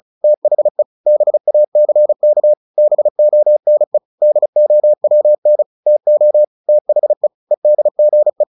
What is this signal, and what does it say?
Morse code, heard as a keyed tone, about 34 wpm, 600 Hz. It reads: THE BACK BONE DOWN TO THE EDGE
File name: the old man and the sea / the old man and the sea 1 35wpm